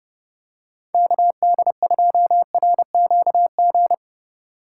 Morse code, heard as a keyed tone, about 28 wpm, 700 Hz. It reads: KD2RQG